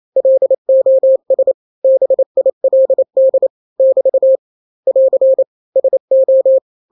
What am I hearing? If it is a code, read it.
Morse code, heard as a keyed tone, about 28 wpm, 530 Hz. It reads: LOS BILD <BT> <AR> SO
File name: daily / 27wpm